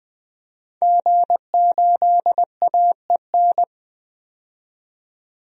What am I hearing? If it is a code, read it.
Morse code, heard as a keyed tone, about 20 wpm, 700 Hz. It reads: G8AEN
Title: DXCC-100 Call Signs - 5 Characters - 20wpm